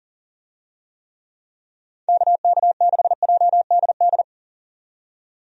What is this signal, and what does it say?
Morse code, heard as a keyed tone, about 40 wpm, 700 Hz. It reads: KK6JDD
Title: USA Call Signs - 6 Characters - 40wpm